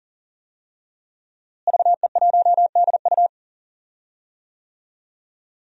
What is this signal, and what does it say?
Morse code, heard as a keyed tone, about 40 wpm, 700 Hz. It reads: VE1DU
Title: DXCC-100 Call Signs - 5 Characters - Repeated 2x - 40wpm